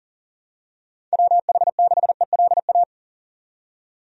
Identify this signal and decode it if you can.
Morse code, heard as a keyed tone, about 37 wpm, 700 Hz. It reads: WH6ELA
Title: USA Call Signs - 6 Characters - 40wpm